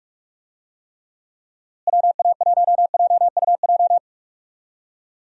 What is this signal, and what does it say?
Morse code, heard as a keyed tone, about 46 wpm, 700 Hz. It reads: WA1JUJ
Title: USA Call Signs - 6 Characters - Repeated 3x - 45wpm